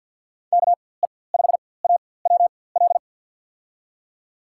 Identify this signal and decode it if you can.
Morse code, heard as a keyed tone, about 52 wpm, spaced as for 23 wpm, 700 Hz. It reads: KE5AWL
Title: USA Call Signs - 6 Characters - 4x Character Spacing - 50wpm